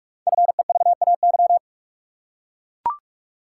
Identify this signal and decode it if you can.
Morse code, heard as a keyed tone, about 44 wpm, 700 Hz. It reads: FEVAY
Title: Code Groups - 5 Letter:Numbers - 45wpm